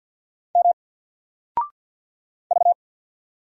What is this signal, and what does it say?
Morse code, heard as a keyed tone, about 48 wpm, 700 Hz. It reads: M V
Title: Single Letter-Number Alt 50wpm